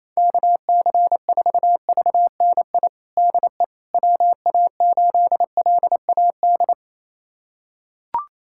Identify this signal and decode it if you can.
Morse code, heard as a keyed tone, about 28 wpm, 700 Hz. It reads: KC4VNI DE WA8LAD
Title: USA Callsign DE Callsign 28wpm